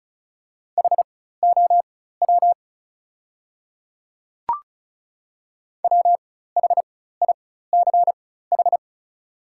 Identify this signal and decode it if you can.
Morse code, heard as a keyed tone, about 34 wpm, spaced as for 16 wpm, 690 Hz. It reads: HOW WHICH
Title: 4x Character Spacing - Top 100 Words - 35wpm